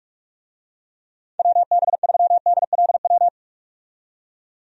Morse code, heard as a keyed tone, about 41 wpm, 700 Hz. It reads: WB3DLW
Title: USA Call Signs - 6 Characters - 45wpm